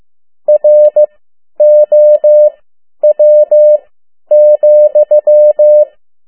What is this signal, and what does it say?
Morse code, heard as a keyed tone, about 15 wpm, spaced as for 10 wpm, 600 Hz. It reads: ROW,
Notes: Around the World in 80 Days paragraph one Morse code 10 WPM